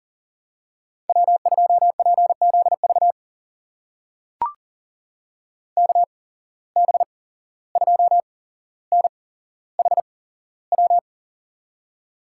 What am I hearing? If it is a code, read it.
Morse code, heard as a keyed tone, about 42 wpm, spaced as for 16 wpm, 700 Hz. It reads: W2PZV KB2NHW
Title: USA Call Signs - 8x Character Spacing - 40wpm